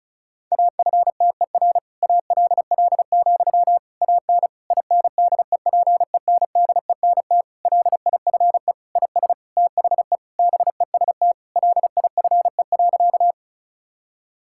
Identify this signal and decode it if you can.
Morse code, heard as a keyed tone, about 36 wpm, 700 Hz. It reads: AFTER ALL, AN INDEPENDENT LIFE IS THE BEST LIFE.
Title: Sentences from Top 1000 Words - Review Part 6 - 35wpm